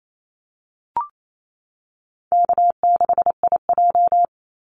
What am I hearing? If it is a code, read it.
Morse code, heard as a keyed tone, about 28 wpm, 700 Hz. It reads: K6IJ
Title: Easy Call Signs Repeated 2x - 28wpm